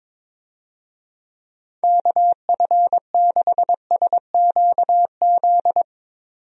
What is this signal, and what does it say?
Morse code, heard as a keyed tone, about 22 wpm, 700 Hz. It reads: KF6SQZ
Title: USA Call Signs - 6 Characters - 22wpm